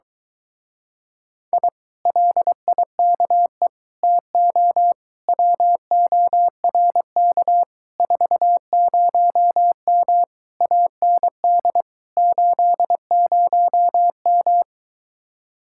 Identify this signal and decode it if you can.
Morse code, heard as a keyed tone, about 23 wpm, 700 Hz. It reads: I LIKE TO WORK 40M AND 80M